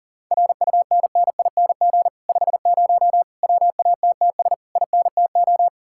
Morse code, heard as a keyed tone, about 51 wpm, 700 Hz. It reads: RUNNING 50 WATTS INTO